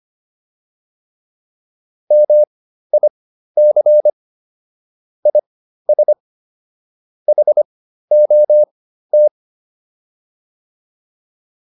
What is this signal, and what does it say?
Morse code, heard as a keyed tone, about 25 wpm, spaced as for 13 wpm, 600 Hz. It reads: MIC IS HOT